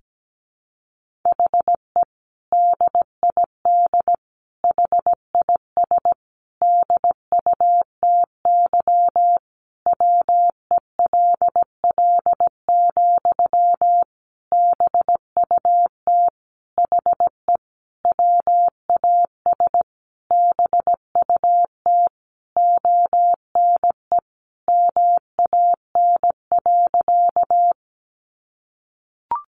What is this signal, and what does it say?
Morse code, heard as a keyed tone, about 17 wpm, 700 Hz. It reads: HE DID HIS DUTY WELL, BUT HE WAS BUT ONE MAN.
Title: Sentences from Top 1500 Words 17wpm